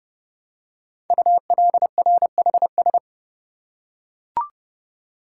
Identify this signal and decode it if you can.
Morse code, heard as a keyed tone, about 28 wpm, 700 Hz. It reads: ULRHS
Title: Code Groups - 5 Letters - 30wpm